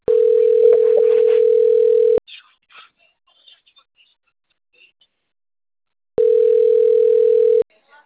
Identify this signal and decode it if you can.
Ringback tone (North America), 440 + 480 Hz, with speech.
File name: out- - - - - .19027